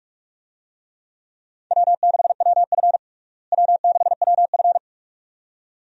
Morse code, heard as a keyed tone, about 46 wpm, 700 Hz. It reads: W6WF W6WF
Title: USA Call Signs - 4 Characters - Repeated 2x - 45wpm